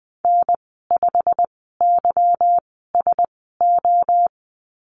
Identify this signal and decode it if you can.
Morse code, heard as a keyed tone, about 20 wpm, spaced as for 14 wpm, 700 Hz. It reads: N5YSO
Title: 179 Lesson 29g - Slash - Call Signs 20wpm